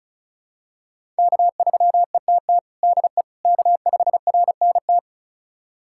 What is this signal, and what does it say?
Morse code, heard as a keyed tone, about 34 wpm, 700 Hz. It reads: K3ETT DE K5RNT
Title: USA Callsign DE Callsign - 5 Characters - 35wpm